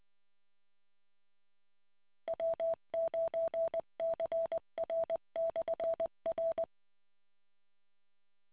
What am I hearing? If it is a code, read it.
Morse code, heard as a keyed tone, about 20 wpm, 660 Hz. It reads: W9CR/R